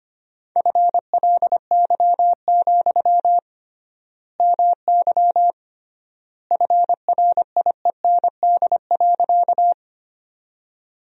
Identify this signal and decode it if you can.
Morse code, heard as a keyed tone, about 25 wpm, 700 Hz. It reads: FLY, MY FRIEND.
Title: 3x Word-Spacing - Sentences from Top 500 Words - 25wpm